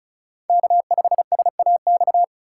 Morse code, heard as a keyed tone, about 36 wpm, 700 Hz. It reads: K5SAX
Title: USA Call Signs - 5 Characters - 4x Character Spacing - 35wpm